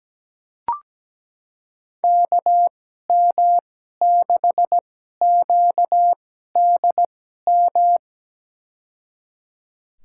Morse code, heard as a keyed tone, about 17 wpm, spaced as for 12 wpm, 700 Hz. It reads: KM6QDM
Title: USA Call Signs - 6 Characters - 2x Character Spacing - 17wpm